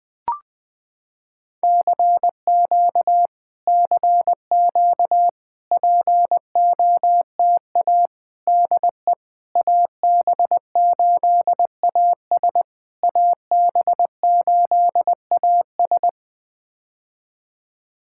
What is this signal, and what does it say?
Morse code, heard as a keyed tone, about 20 wpm, 700 Hz. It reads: CQ CQ POTA DE AB8AS AB8AS
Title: POTA QSO Practice - 20wpm